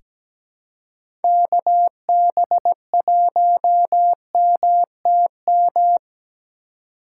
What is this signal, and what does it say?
Morse code, heard as a keyed tone, about 17 wpm, 700 Hz. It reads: KB1MTM
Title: USA Call Signs - 6 Characters - 4x Character Spacing - 17wpm